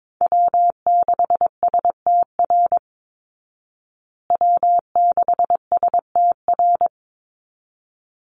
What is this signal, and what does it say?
Morse code, heard as a keyed tone, about 22 wpm, 700 Hz. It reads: W6STR W6STR
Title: Call Signs Repeated 3x - 22wpm